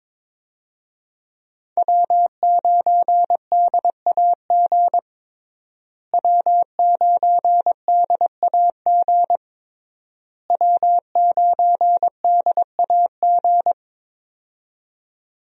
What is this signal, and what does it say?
Morse code, heard as a keyed tone, about 22 wpm, 700 Hz. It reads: W9DAG W9DAG W9DAG
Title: USA Call Signs - 5 Characters - Repeated 3x - 22wpm